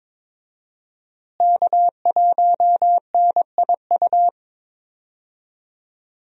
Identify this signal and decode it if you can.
Morse code, heard as a keyed tone, about 21 wpm, 700 Hz. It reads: K1NIU
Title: USA Call Signs - 5 Characters - Repeated 2x - 22wpm